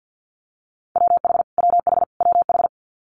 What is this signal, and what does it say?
Morse code, heard as a keyed tone, about 50 wpm, 700 Hz. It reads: WH WH WH
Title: Multi-letter Phonemes Repeated 3x 50wpm